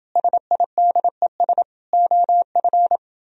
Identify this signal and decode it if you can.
Morse code, heard as a keyed tone, about 28 wpm, 700 Hz. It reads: SIDES OF